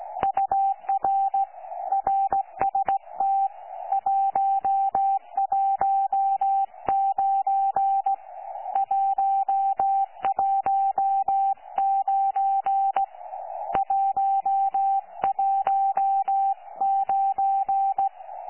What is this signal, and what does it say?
Morse code, heard as a keyed tone, about 16 wpm, 780 Hz. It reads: UR RST 119 119 119
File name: 12 45Z 18088.1kHz